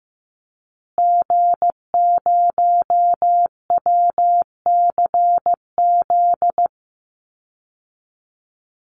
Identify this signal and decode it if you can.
Morse code, heard as a keyed tone, about 15 wpm, 700 Hz. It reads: G0WCZ
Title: Call Signs Repeated 3x - 15wpm